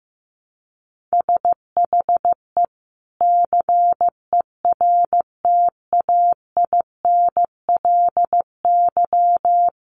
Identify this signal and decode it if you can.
Morse code, heard as a keyed tone, about 15 wpm, 700 Hz. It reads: SHE CERTAINLY
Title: Sentences from Top 1100 Words 15wpm